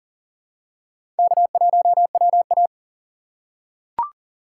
Morse code, heard as a keyed tone, about 40 wpm, 700 Hz. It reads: K1WA
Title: USA Call Signs - 4 Characters - 3x Character Spacing - 40wpm